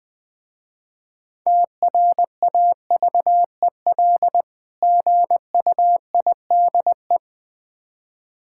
Morse code, heard as a keyed tone, about 20 wpm, 700 Hz. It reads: TRAVEL GUIDE
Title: Difficult Words with Hints - 20wpm